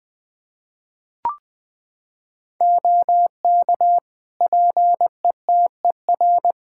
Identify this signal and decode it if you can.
Morse code, heard as a keyed tone, about 20 wpm, 700 Hz. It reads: OK PETER
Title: Practice QSO - Exchange 4 - Ending - 20wpm